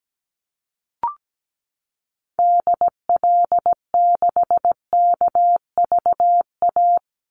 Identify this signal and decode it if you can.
Morse code, heard as a keyed tone, about 17 wpm, 700 Hz. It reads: DL6KVA
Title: Call Signs Repeated 2x - 17wpm